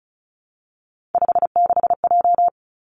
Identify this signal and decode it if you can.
Morse code, heard as a keyed tone, about 36 wpm, 700 Hz. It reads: 56J
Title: Three Challenging Letter-Number Combinations 35wpm